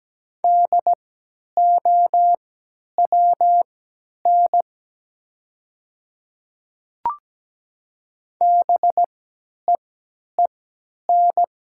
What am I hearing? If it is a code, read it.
Morse code, heard as a keyed tone, about 17 wpm, spaced as for 10 wpm, 700 Hz. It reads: DOWN BEEN
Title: 3x Character Spacing - Top 100 Words - 17wpm